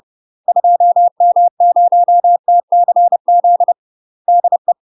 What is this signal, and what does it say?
Morse code, heard as a keyed tone, about 30 wpm, 700 Hz. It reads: 2M0TCZ DE
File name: UK MASTER QSOs TRAINING/30WPM